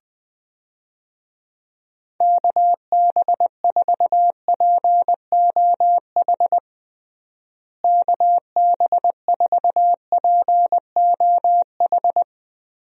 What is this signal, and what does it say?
Morse code, heard as a keyed tone, about 20 wpm, 700 Hz. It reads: KB4POH KB4POH
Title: USA Call Signs - 6 Characters - Repeated 3x - 20wpm